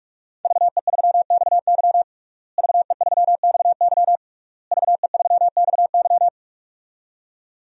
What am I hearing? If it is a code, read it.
Morse code, heard as a keyed tone, about 46 wpm, 700 Hz. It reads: VE3XY VE3XY VE3XY
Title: DXCC-100 Call Signs - 5 Characters - Repeated 3x - 45wpm